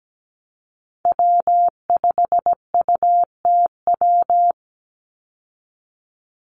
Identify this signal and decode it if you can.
Morse code, heard as a keyed tone, about 17 wpm, 700 Hz. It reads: W5UTW
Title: Call Signs Repeated 3x - 17wpm